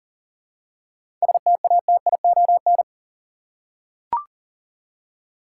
Morse code, heard as a keyed tone, about 37 wpm, 700 Hz. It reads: STATION
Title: Seven Letter Words - 40wpm